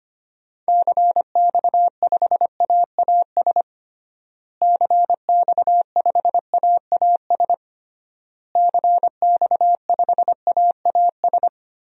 Morse code, heard as a keyed tone, about 25 wpm, 700 Hz. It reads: CX5AAS CX5AAS CX5AAS
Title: DXCC-100 Call Signs - 6 Characters - Repeated 3x - 25wpm